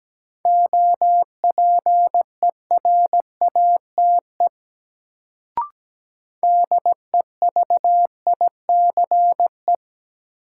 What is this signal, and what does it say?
Morse code, heard as a keyed tone, about 17 wpm, 700 Hz. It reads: OPERATE DEVICE
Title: Sentences from Top 1500 Words - New Word Drill - 17wpm